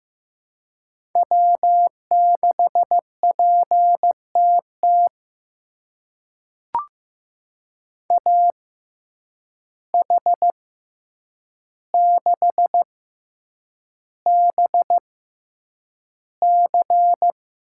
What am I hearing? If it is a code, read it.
Morse code, heard as a keyed tone, about 15 wpm, spaced as for 8 wpm, 700 Hz. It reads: W6PTT AH6BC
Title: USA Call Signs - 5 Characters - 6x Character Spacing - 15wpm